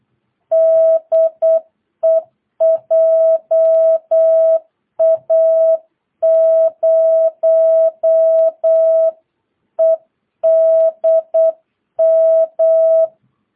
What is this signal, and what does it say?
Morse code, heard as a keyed tone, about 8 wpm, 650 Hz. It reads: DE JA0EDM
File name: jh8cbh / 2022cw